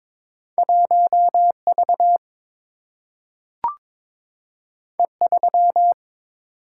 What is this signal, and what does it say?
Morse code, heard as a keyed tone, about 22 wpm, 700 Hz. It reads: 1 V E 3
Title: Single-Letter Single-Number 22wpm